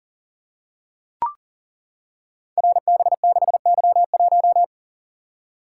Morse code, heard as a keyed tone, about 42 wpm, 700 Hz. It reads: RB6Y1
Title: Code Groups - 5 Letter:Numbers - 40wpm